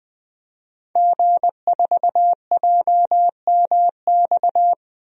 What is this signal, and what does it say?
Morse code, heard as a keyed tone, about 20 wpm, 700 Hz. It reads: G4JMX
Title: DXCC-100 Call Signs - 5 Characters - Repeated 3x - 20wpm